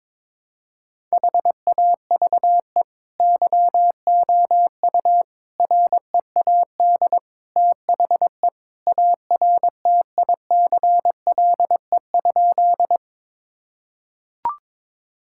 Morse code, heard as a keyed tone, about 22 wpm, 700 Hz. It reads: HAVE YOU READ THE ARTICLE?
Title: Sentences from Top 1000 Words - Review Part 5 - 22wpm